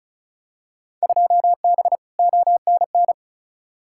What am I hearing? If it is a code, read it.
Morse code, heard as a keyed tone, about 36 wpm, 700 Hz. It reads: 2B ONN
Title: Field Day Exchanges Encore - 35wpm